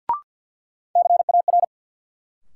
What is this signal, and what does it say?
Morse code, heard as a keyed tone, about 52 wpm, 700 Hz. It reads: CAR
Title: Sets of 5 Words Encore - New Word Drill - 50wpm